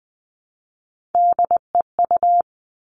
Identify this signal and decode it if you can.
Morse code, heard as a keyed tone, about 20 wpm, 700 Hz. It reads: DEU
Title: All Cut Numbers - 3-Digit 20wpm